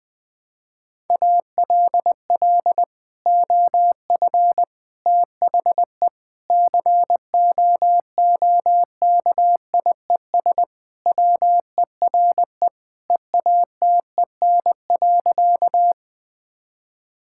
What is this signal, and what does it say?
Morse code, heard as a keyed tone, about 21 wpm, 700 Hz. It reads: ALL OF THE COOKIES WERE EATEN.
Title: Sentences with 2-Word Phrases - 20wpm